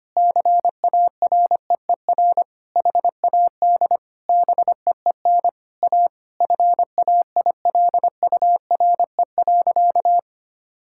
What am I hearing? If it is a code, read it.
Morse code, heard as a keyed tone, about 25 wpm, 700 Hz. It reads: CAREER HAD BEEN A FAILURE.
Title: Sentences from Top 1000 Words - Review Part 6 - 25wpm